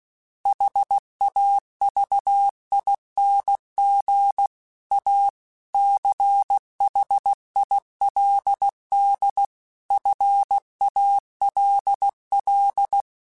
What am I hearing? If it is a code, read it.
Morse code, heard as a keyed tone, about 16 wpm, 800 Hz. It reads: HAVING A CHILD FALL
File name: Quotes / Quotes-mp3s / ChildAsleep @15